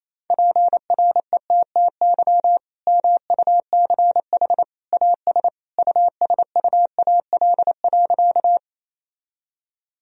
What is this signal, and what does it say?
Morse code, heard as a keyed tone, about 28 wpm, 700 Hz. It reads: PRETTY MUCH AS USUAL.
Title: Sentences from Top 1000 Words - Review Part 2 - 28wpm